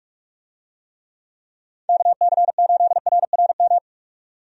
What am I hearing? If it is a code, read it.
Morse code, heard as a keyed tone, about 46 wpm, 700 Hz. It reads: KC8RRM
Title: USA Call Signs - 6 Characters - 45wpm